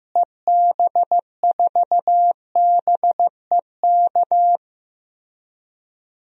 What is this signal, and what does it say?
Morse code, heard as a keyed tone, about 15 wpm, 700 Hz. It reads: EB4BEK
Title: DXCC-100 Call Signs - 6 Characters - Repeated 3x - 15wpm